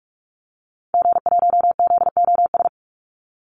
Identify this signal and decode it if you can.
Morse code, heard as a keyed tone, about 44 wpm, 700 Hz. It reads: G1ZOS
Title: Call Signs - 45wpm